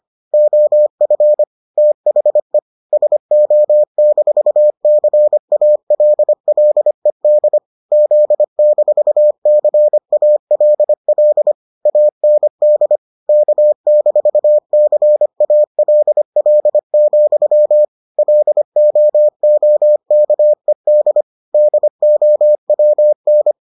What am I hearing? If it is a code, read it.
Morse code, heard as a keyed tone, about 25 wpm, 600 Hz. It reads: OF THE SO-CALLED Z-CALL AND K-CALL, LOOKED DOWN